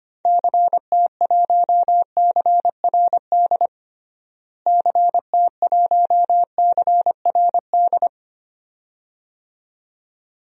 Morse code, heard as a keyed tone, about 25 wpm, 700 Hz. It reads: CT1CRD CT1CRD
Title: DXCC-100 Call Signs - 6 Characters - Repeated 2x - 25wpm